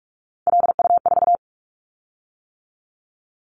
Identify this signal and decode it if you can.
Morse code, heard as a keyed tone, about 46 wpm, 700 Hz. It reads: LU4
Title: Callsign Prefixes for Top 50 DXCC - 45wpm